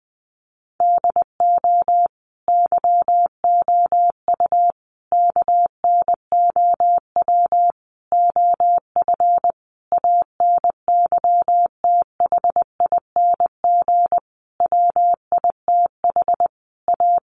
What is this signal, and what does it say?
Morse code, heard as a keyed tone, about 20 wpm, 700 Hz. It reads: DO YOU KNOW OF ANYTHING WITH A
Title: Sentences from Top 1200 Words 20wpm